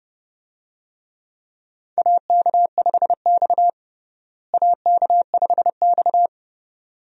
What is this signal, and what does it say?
Morse code, heard as a keyed tone, about 29 wpm, 700 Hz. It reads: AK5X AK5X
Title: USA Call Signs - 4 Characters - Repeated 2x - 30wpm